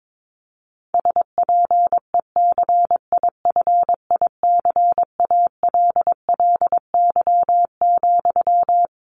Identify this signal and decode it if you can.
Morse code, heard as a keyed tone, about 22 wpm, 700 Hz. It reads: SPECIFICALLY,
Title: Sentences from Top 1100 Words 22wpm